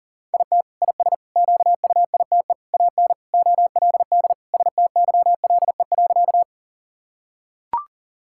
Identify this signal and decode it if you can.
Morse code, heard as a keyed tone, about 37 wpm, 700 Hz. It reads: IT IS QUITE AN OLD STYLE.
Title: Sentences from Top 1000 Words - Review Part 5 - 40wpm